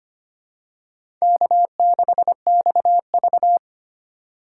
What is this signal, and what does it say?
Morse code, heard as a keyed tone, about 25 wpm, 700 Hz. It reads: K6XV
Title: USA Call Signs - 4 Characters - 3x Character Spacing - 25wpm